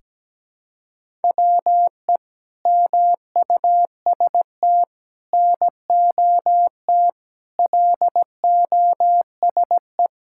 Morse code, heard as a keyed tone, about 17 wpm, 700 Hz. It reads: WE MUST NOT LOSE
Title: Sentences from Top 1600 Words 17wpm